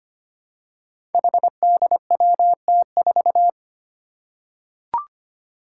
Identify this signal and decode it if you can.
Morse code, heard as a keyed tone, about 25 wpm, 700 Hz. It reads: HDWT4
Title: Five Challenging Letter-Number Combinations 25wpm